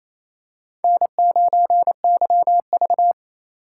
Morse code, heard as a keyed tone, about 28 wpm, 700 Hz. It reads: N9YV
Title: USA Call Signs - 4 Characters - 8x Character Spacing - 28wpm